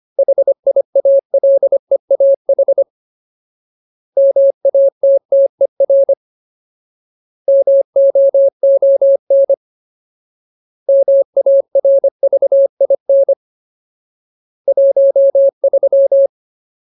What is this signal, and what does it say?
Morse code, heard as a keyed tone, about 25 wpm, 550 Hz. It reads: HIALEAH MATTER MOON MARVIN 13
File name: Ultimate Word List / 25WPM Ultimate